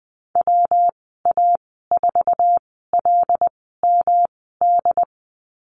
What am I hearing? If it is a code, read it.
Morse code, heard as a keyed tone, about 20 wpm, spaced as for 14 wpm, 700 Hz. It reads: WA4LMD
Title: 179 Lesson 29g - Slash - Call Signs 20wpm